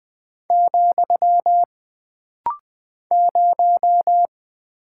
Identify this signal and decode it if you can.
Morse code, heard as a keyed tone, about 20 wpm, 700 Hz. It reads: , 0
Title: Single Character (Letter:Number:Punctuation:Prosign) - Rapid-Fire - 20wpm